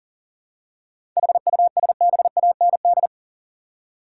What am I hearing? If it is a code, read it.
Morse code, heard as a keyed tone, about 40 wpm, 700 Hz. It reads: HUSBAND
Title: Seven Letter Words - 40wpm